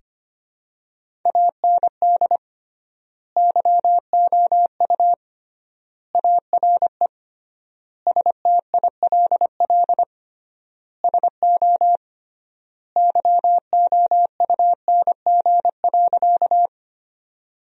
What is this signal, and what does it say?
Morse code, heard as a keyed tone, about 25 wpm, 700 Hz. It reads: AND YOU ARE STILL SO YOUNG.
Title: 3x Word-Spacing - Sentences from Top 400 Words - 25wpm